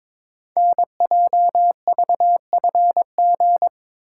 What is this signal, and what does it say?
Morse code, heard as a keyed tone, about 22 wpm, 700 Hz. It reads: NJVFG
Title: Code Groups - 5 Letters - 22wpm